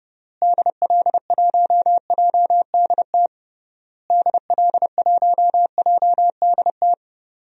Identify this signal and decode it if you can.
Morse code, heard as a keyed tone, about 32 wpm, 700 Hz. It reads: DL1JDT DL1JDT
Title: DXCC-100 Call Signs - 6 Characters - Repeated 3x - 30wpm